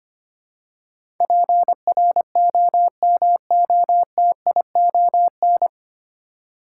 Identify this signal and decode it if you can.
Morse code, heard as a keyed tone, about 25 wpm, 700 Hz. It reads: PROMOTION
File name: Nine Letter Words - 25wpm